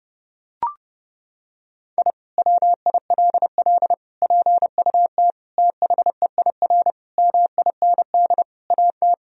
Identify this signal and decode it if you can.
Morse code, heard as a keyed tone, about 30 wpm, 700 Hz. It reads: I WILL PUT THEIR MIND AT
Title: Sentences from Top 1000 Words - Review Part 1 - 30wpm